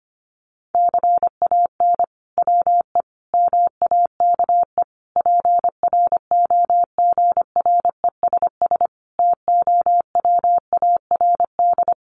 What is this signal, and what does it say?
Morse code, heard as a keyed tone, about 25 wpm, 700 Hz. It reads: CAN WE MAKE PROGRESS TOWARD
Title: Sentences from Top 1500 Words 25wpm